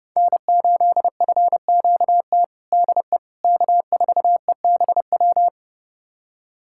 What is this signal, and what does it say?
Morse code, heard as a keyed tone, about 30 wpm, 700 Hz. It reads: N8FQT DE K4EBW
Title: USA Callsign DE Callsign - 5 Characters - 30wpm